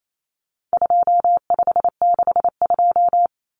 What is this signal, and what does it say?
Morse code, heard as a keyed tone, about 28 wpm, 700 Hz. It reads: 2562
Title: Common Cut Numbers - 4-Digit 28wpm